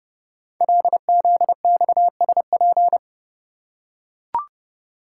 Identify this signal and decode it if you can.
Morse code, heard as a keyed tone, about 30 wpm, 690 Hz. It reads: LZXSP
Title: Code Groups - 5 Letters - 30wpm